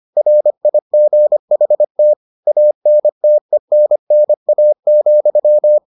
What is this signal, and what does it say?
Morse code, heard as a keyed tone, about 25 wpm, 600 Hz. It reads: RIGHT ANTENNA,